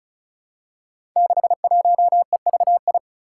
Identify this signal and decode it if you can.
Morse code, heard as a keyed tone, about 36 wpm, 700 Hz. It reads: 61EVI
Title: Five Challenging Letter-Number Combinations 35wpm